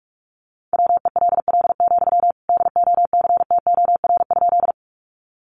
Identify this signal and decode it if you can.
Morse code, heard as a keyed tone, about 44 wpm, 700 Hz. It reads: WELL, DOCTOR?
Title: Sentences from Top 1500 Words 45wpm